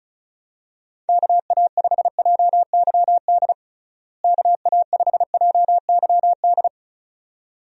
Morse code, heard as a keyed tone, about 34 wpm, 700 Hz. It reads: KA5JYD KA5JYD
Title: USA Call Signs - 6 Characters - Repeated 2x - 35wpm